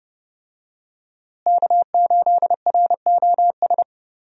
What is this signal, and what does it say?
Morse code, heard as a keyed tone, about 29 wpm, 700 Hz. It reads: K8ROS
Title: USA Call Signs - 5 Characters - Repeated 3x - 30wpm